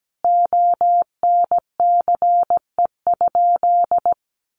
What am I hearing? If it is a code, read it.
Morse code, heard as a keyed tone, about 17 wpm, 700 Hz. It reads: ONCE?
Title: Sentences from Top 1100 Words 17wpm